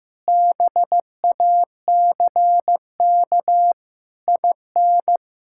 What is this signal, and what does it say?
Morse code, heard as a keyed tone, about 16 wpm, 700 Hz. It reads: BACK IN
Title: Pangram Challenge v2 - 15wpm